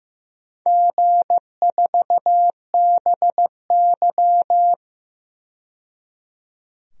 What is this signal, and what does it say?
Morse code, heard as a keyed tone, about 15 wpm, 700 Hz. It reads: G4BY
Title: DXCC-100 Call Signs - 4 Characters - 15wpm